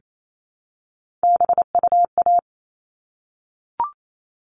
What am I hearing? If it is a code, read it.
Morse code, heard as a keyed tone, about 28 wpm, 700 Hz. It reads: BUA
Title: All Cut Numbers - 3-Digit 28wpm